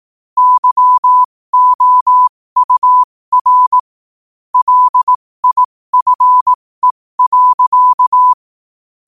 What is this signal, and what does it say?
Morse code, heard as a keyed tone, about 18 wpm, spaced as for 15 wpm, 1000 Hz. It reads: YOUR LIFE.
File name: qotdfiles / 15wpm